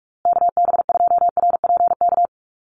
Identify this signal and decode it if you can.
Morse code, heard as a keyed tone, about 44 wpm, 700 Hz. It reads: KB2RPK
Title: USA Call Signs Repeated 2x 45wpm